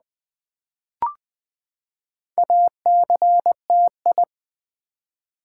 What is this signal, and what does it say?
Morse code, heard as a keyed tone, about 21 wpm, 700 Hz. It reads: ACTI
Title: Build A Word - 4 Letters At A Time - 8-Letter Words - 20wpm